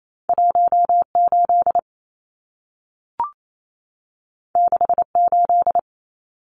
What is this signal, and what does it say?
Morse code, heard as a keyed tone, about 28 wpm, 700 Hz. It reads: 18 68
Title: Numbers - 2-Digit 28wpm